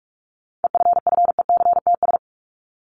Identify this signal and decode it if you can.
Morse code, heard as a keyed tone, about 44 wpm, 700 Hz. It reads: EFFECTS
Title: ES:S Suffix - Top 100 Words with Base Words 45wpm